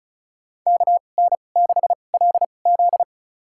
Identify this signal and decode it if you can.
Morse code, heard as a keyed tone, about 36 wpm, spaced as for 25 wpm, 700 Hz. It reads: KN6LZ
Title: USA Call Signs - 2x Character Spacing - 35wpm